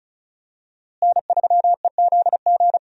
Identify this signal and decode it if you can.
Morse code, heard as a keyed tone, about 34 wpm, 700 Hz. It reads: N3EZG
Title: USA Call Signs - 5 Characters - Repeated 3x - 35wpm